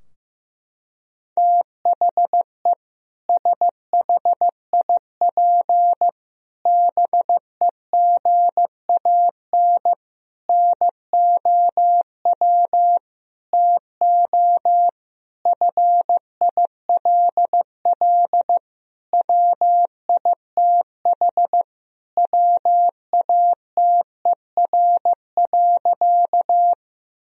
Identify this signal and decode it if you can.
Morse code, heard as a keyed tone, about 15 wpm, 700 Hz. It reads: THE SHIP BEGAN NOW TO FILL WITH WATER.
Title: Sentences from Top 1000 Words - Review Part 3 - 15wpm